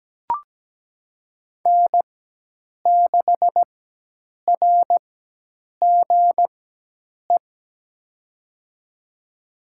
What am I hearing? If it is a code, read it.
Morse code, heard as a keyed tone, about 17 wpm, spaced as for 8 wpm, 700 Hz. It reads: N6RGE
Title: USA Call Signs - 5 Characters - 4x Character Spacing - 17wpm